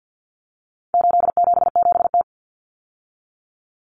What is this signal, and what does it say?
Morse code, heard as a keyed tone, about 50 wpm, 700 Hz. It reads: 877T
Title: Common Cut Numbers - 4-Digit 50wpm